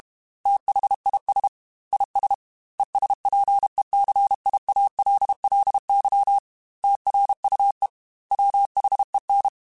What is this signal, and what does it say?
Morse code, heard as a keyed tone, about 32 wpm, 800 Hz. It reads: THIS IS ESPECIALLY TRUE WHEN
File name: Technical Subjects / Technical Subjects-mp3s / AND Gate @30